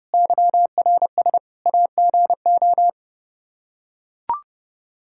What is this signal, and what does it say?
Morse code, heard as a keyed tone, about 32 wpm, 700 Hz. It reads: YRS AGO
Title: K1USN SST QSO Practice 30wpm